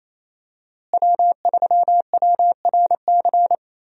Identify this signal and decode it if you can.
Morse code, heard as a keyed tone, about 28 wpm, 700 Hz. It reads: W3WRC
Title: USA Call Signs - 5 Characters - Repeated 2x - 28wpm